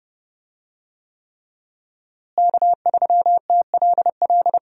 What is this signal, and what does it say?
Morse code, heard as a keyed tone, about 30 wpm, 700 Hz. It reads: K3TLL
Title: USA Call Signs - 5 Characters - Repeated 3x - 30wpm